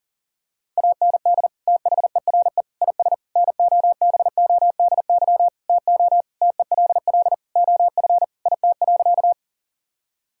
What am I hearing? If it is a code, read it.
Morse code, heard as a keyed tone, about 40 wpm, 700 Hz. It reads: AND THERE IS NOBODY TO TELL OF IT.
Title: Sentences from Top 1600 Words 40wpm